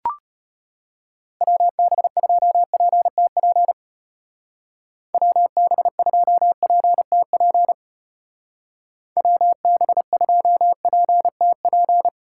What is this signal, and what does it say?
Morse code, heard as a keyed tone, about 32 wpm, 700 Hz. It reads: WB2PTP WB2PTP WB2PTP